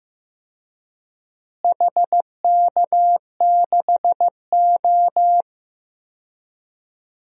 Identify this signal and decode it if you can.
Morse code, heard as a keyed tone, about 15 wpm, 700 Hz. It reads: HK6O
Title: DXCC-100 Call Signs - 4 Characters - Repeated 3x - 15wpm